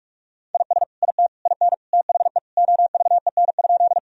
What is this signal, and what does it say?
Morse code, heard as a keyed tone, about 46 wpm, 700 Hz. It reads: IS IT IN THE OVEN?
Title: Sentences from Top 1000 Words - Review Part 4 - 45wpm